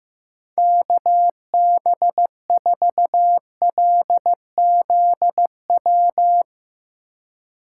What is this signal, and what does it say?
Morse code, heard as a keyed tone, about 15 wpm, 700 Hz. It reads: KB4LZW
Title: USA Call Signs - 3x Character Spacing - 15wpm